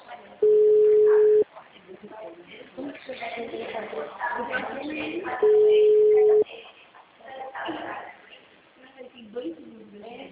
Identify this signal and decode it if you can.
Ringback tone (Europe, 425 Hz), with speech.